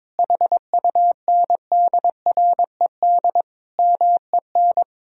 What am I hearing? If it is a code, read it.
Morse code, heard as a keyed tone, about 22 wpm, 700 Hz. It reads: HUNDRED MEN
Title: Sentences from Top 1000 Words - Review Part 2 - 22wpm